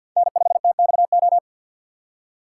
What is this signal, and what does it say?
Morse code, heard as a keyed tone, about 50 wpm, 700 Hz. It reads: N5TXO
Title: USA Call Signs - 5 Characters - 6x Character Spacing - 50wpm